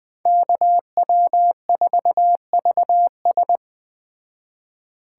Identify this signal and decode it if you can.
Morse code, heard as a keyed tone, about 20 wpm, 700 Hz. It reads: KW4VS